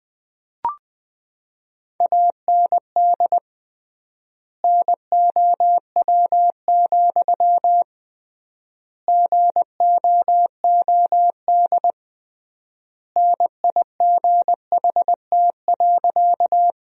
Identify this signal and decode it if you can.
Morse code, heard as a keyed tone, about 20 wpm, 700 Hz. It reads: AND NOW, GOOD NIGHT.
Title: 3x Word-Spacing - Sentences from Top 300 Words - 20wpm